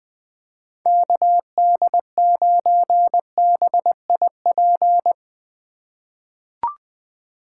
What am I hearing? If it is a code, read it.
Morse code, heard as a keyed tone, about 21 wpm, 700 Hz. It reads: KD9BIP
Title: USA Call Signs - 6 Characters - 3x Character Spacing - 20wpm